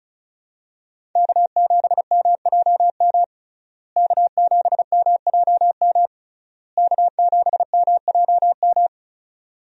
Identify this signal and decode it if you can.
Morse code, heard as a keyed tone, about 36 wpm, 700 Hz. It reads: K7MJM K7MJM K7MJM
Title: USA Call Signs - 5 Characters - Repeated 3x - 35wpm